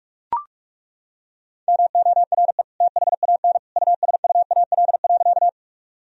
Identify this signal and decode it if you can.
Morse code, heard as a keyed tone, about 41 wpm, 700 Hz. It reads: MORE THAN USUAL.
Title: Sentences from Top 1000 Words - Review Part 2 - 45wpm